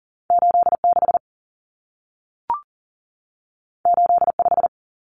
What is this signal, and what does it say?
Morse code, heard as a keyed tone, about 40 wpm, 700 Hz. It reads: 86 85
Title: Numbers - 2-Digit 40wpm